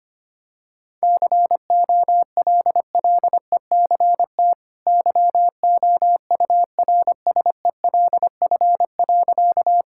Morse code, heard as a keyed tone, about 25 wpm, 700 Hz. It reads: COLLECT YOURSELF.
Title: Sentences from Top 1600 Words 25wpm